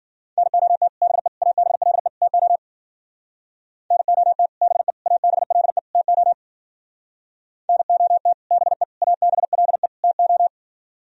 Morse code, heard as a keyed tone, about 51 wpm, 700 Hz. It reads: NOT BE ABLE TO NOT BE ABLE TO NOT BE ABLE TO